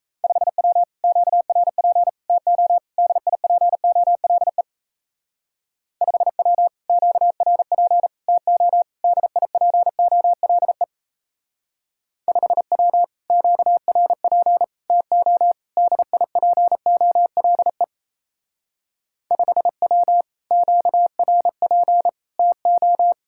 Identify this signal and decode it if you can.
Morse code, heard as a keyed tone, about 33 wpm, 700 Hz. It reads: 5W QRP TO DIPOLE 5W QRP TO DIPOLE 5W QRP TO DIPOLE 5W QRP TO
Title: Speed-Racing Common QSO Phrases Encore - 28wpm